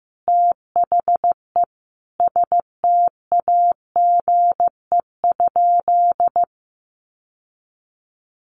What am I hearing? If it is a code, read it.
Morse code, heard as a keyed tone, about 15 wpm, 700 Hz. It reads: THE STAGE?
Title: Sentences from Top 1200 Words 15wpm